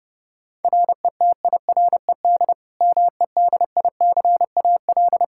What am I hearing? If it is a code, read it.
Morse code, heard as a keyed tone, about 30 wpm, 700 Hz. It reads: RETIRED MEDICAL
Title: K1USN SST QSO Practice 30wpm